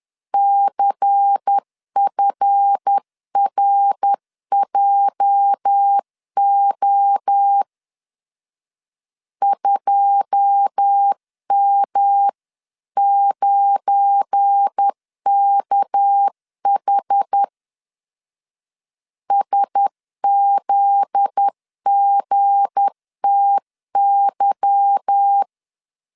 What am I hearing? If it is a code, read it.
Morse code, heard as a keyed tone, about 10 wpm, 800 Hz. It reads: CFRJO 2M9KH SZGTY
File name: LEZIONI / 9 28 10wpm 2842